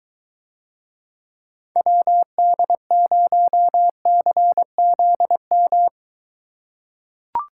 Morse code, heard as a keyed tone, about 22 wpm, 700 Hz. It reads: WD0CZM